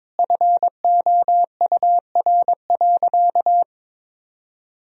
Morse code, heard as a keyed tone, about 22 wpm, 700 Hz. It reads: FOUR.
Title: Sentences from Top 1000 Words - Review Part 3 - 22wpm